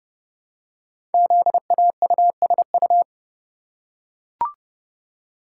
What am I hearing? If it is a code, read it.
Morse code, heard as a keyed tone, about 30 wpm, 700 Hz. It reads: ZAUSU
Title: Code Groups - 5 Letters - 30wpm